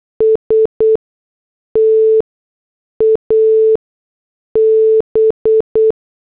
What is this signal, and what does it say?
Morse code, heard as a keyed tone, about 8 wpm, spaced as for 6 wpm, 440 Hz. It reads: S TAB